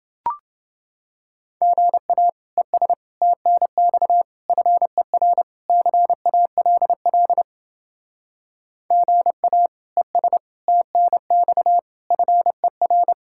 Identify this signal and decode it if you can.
Morse code, heard as a keyed tone, about 29 wpm, 700 Hz. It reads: GA ES TNX FER CALL GA ES TNX FER